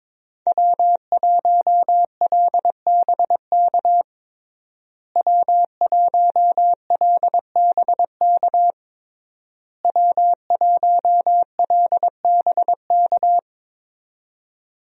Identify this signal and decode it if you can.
Morse code, heard as a keyed tone, about 22 wpm, 700 Hz. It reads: W1LBK W1LBK W1LBK
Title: USA Call Signs - 5 Characters - Repeated 3x - 22wpm